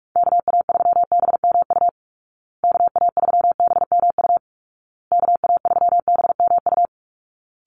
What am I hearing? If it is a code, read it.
Morse code, heard as a keyed tone, about 44 wpm, 700 Hz. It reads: KA3BMU KA3BMU KA3BMU
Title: Call Signs Repeated 2x - 45wpm